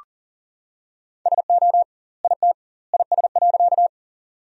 Morse code, heard as a keyed tone, about 39 wpm, 700 Hz. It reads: SO IT IS.
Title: 2x Word-Spacing - Sentences from Top 100 Words - 40wpm